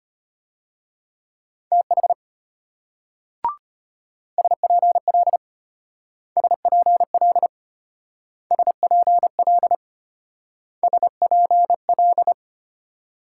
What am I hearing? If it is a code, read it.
Morse code, heard as a keyed tone, about 33 wpm, 700 Hz. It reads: TH SPL SPL SPL SPL